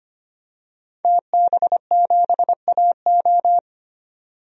Morse code, heard as a keyed tone, about 25 wpm, 700 Hz. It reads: TB7AO
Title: DXCC-100 Call Signs - 5 Characters - 25wpm